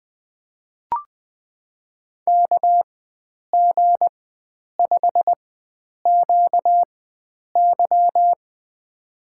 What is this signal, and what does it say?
Morse code, heard as a keyed tone, about 20 wpm, spaced as for 9 wpm, 700 Hz. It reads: KG5QY
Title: USA Call Signs - 5 Characters - 4x Character Spacing - 20wpm